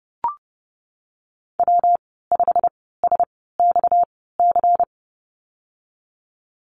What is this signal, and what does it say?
Morse code, heard as a keyed tone, about 30 wpm, spaced as for 17 wpm, 700 Hz. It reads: W5SXC
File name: Lesson 37g - Letter x - Call Signs 30wpm